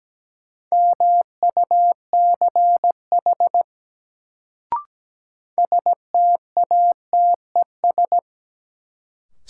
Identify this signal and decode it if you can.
Morse code, heard as a keyed tone, about 17 wpm, 700 Hz. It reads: MUCH STATES
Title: Sets of 2 Words - New Word Drill - 17wpm